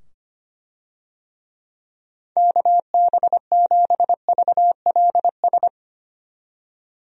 Morse code, heard as a keyed tone, about 25 wpm, 700 Hz. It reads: KB7VLS
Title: USA Call Signs - 6 Characters - 25wpm